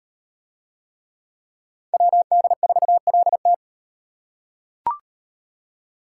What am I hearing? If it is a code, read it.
Morse code, heard as a keyed tone, about 38 wpm, 700 Hz. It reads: WD4LT